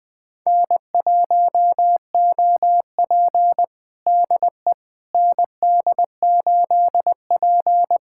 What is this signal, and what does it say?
Morse code, heard as a keyed tone, about 21 wpm, 700 Hz. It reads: N1OP DE ND8P
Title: USA Callsign DE Callsign - 4 Characters - 20wpm